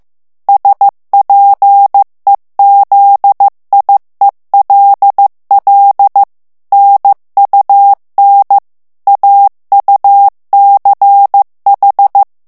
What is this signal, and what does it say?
Morse code, heard as a keyed tone, about 15 wpm, 800 Hz. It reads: SPEZIELL NUN AUCH
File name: morse mp3 / ALLEMAND texte clair 1 vitesse 14 mots minute